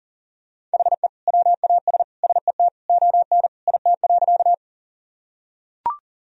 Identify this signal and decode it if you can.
Morse code, heard as a keyed tone, about 37 wpm, 700 Hz. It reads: HE WAS SET ON IT.
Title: Sentences from Top 1000 Words - Review Part 1 - 40wpm